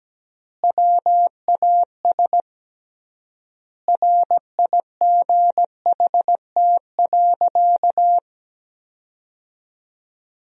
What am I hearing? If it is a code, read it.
Morse code, heard as a keyed tone, about 17 wpm, 700 Hz. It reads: WAS RIGHT.
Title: 3x Word-Spacing - Sentences from Top 500 Words - 17wpm